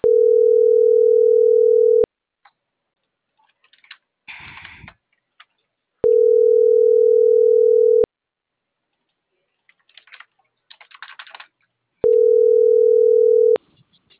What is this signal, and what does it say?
Ringback tone (North America), 440 + 480 Hz.